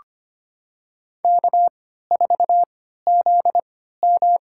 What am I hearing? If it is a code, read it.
Morse code, heard as a keyed tone, about 25 wpm, spaced as for 14 wpm, 700 Hz. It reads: K4ZM
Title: USA Call Signs - 3x Character Spacing - 25wpm